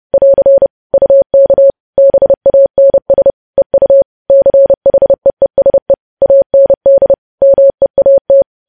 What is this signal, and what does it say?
Morse code, heard as a keyed tone, about 30 wpm, 550 Hz. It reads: <AR> UK BANS EU CHEESE AND MEAT
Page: news headlines in morse code at 30 wpm